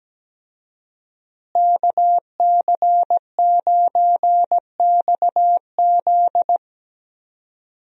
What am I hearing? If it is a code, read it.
Morse code, heard as a keyed tone, about 17 wpm, 700 Hz. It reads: KC9XZ
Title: USA Call Signs - 5 Characters - Repeated 2x - 17wpm